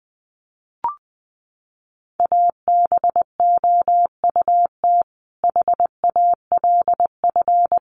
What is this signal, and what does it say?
Morse code, heard as a keyed tone, about 20 wpm, 700 Hz. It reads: ABOUT HALF
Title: Sentences from Top 600 Words 20wpm